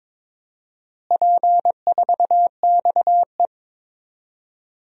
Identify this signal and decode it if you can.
Morse code, heard as a keyed tone, about 21 wpm, 700 Hz. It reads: P4XE
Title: DXCC-100 Call Signs - 4 Characters - Repeated 3x - 22wpm